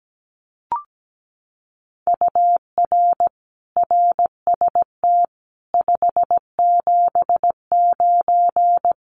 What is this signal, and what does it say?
Morse code, heard as a keyed tone, about 17 wpm, 700 Hz. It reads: UR RST 579
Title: Common QSO Phrases Encore 17wpm